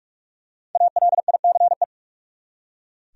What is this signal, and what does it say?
Morse code, heard as a keyed tone, about 43 wpm, 700 Hz. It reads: ALICE
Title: USA Common Names - 45wpm